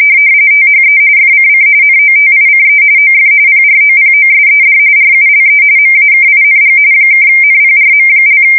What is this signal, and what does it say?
RTTY, about 45.45 baud. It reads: THE FCC NOTICE OF PROPOSED RULE MAKING (NPRM)
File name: fsk170 / teletype-RTTY